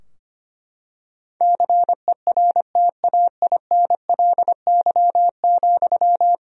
Morse code, heard as a keyed tone, about 25 wpm, 700 Hz. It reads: CERTAINLY,
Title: Sentences from Top 1000 Words - Review Part 3 - 25wpm